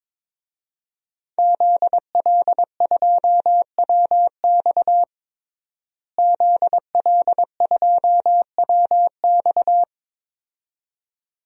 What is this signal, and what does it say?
Morse code, heard as a keyed tone, about 21 wpm, 700 Hz. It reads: ZL2WX ZL2WX
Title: DXCC-100 Call Signs - 5 Characters - Repeated 2x - 22wpm